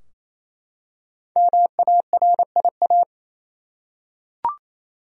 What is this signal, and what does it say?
Morse code, heard as a keyed tone, about 28 wpm, 700 Hz. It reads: MARIA
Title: USA Common Names - 28wpm